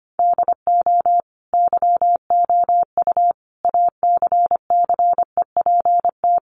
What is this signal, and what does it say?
Morse code, heard as a keyed tone, about 25 wpm, 700 Hz. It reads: DO YOU ACCEPT
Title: Sentences from Top 1300 Words 25wpm